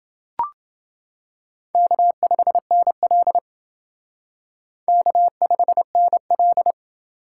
Morse code, heard as a keyed tone, about 28 wpm, 700 Hz. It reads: K5NL K5NL